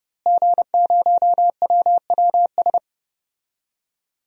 Morse code, heard as a keyed tone, about 30 wpm, 700 Hz. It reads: G0WWS
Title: DXCC-100 Call Signs - 5 Characters - Repeated 2x - 30wpm